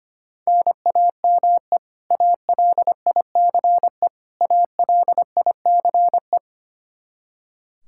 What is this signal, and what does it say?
Morse code, heard as a keyed tone, about 25 wpm, 700 Hz. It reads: NAME ALICE ALICE
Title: Practice QSO - Exchange 1 - Beginning - 25wpm